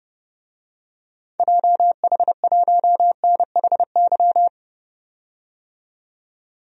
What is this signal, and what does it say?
Morse code, heard as a keyed tone, about 30 wpm, 700 Hz. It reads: JH1NHY
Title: DXCC-100 Call Signs - 6 Characters - 30wpm